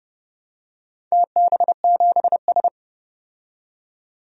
Morse code, heard as a keyed tone, about 29 wpm, 700 Hz. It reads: TB7S
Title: DXCC-100 Call Signs - 4 Characters - 30wpm